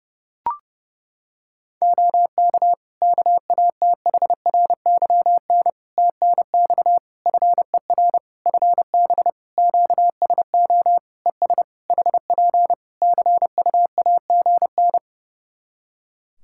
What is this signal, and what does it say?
Morse code, heard as a keyed tone, about 30 wpm, 700 Hz. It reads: OK KATHRYN TNX FER FB QSO ES HP CUAGN
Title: Practice QSO - Exchange 4 - Ending - 30wpm